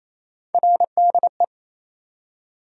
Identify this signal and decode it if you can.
Morse code, heard as a keyed tone, about 27 wpm, 700 Hz. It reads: RDE
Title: USA Call Sign Postfixes - 3 Letters 28wpm